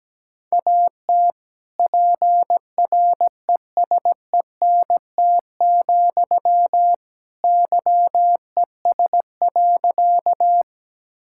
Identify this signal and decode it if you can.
Morse code, heard as a keyed tone, about 17 wpm, 700 Hz. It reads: AT PRESENT, YES.
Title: Sentences from Top 1000 Words - Review Part 5 - 17wpm